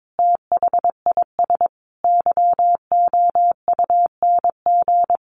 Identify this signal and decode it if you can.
Morse code, heard as a keyed tone, about 22 wpm, 700 Hz. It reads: THIS YOUNG
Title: Sentences from Top 1300 Words 22wpm